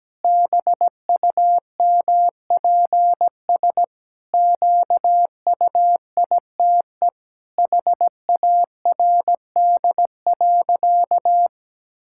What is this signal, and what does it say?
Morse code, heard as a keyed tone, about 17 wpm, 700 Hz. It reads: BUMPS QUITE HARD.
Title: Pangram Challenge v2 - 17wpm